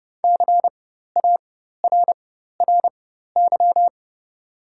Morse code, heard as a keyed tone, about 30 wpm, spaced as for 14 wpm, 700 Hz. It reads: CARRY
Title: 4x Character Spacing - Top 300 Words - 30wpm